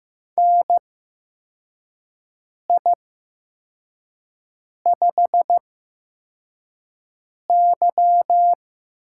Morse code, heard as a keyed tone, about 15 wpm, spaced as for 4 wpm, 700 Hz. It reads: NI5Y
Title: USA Call Signs - 8x Character Spacing - 15wpm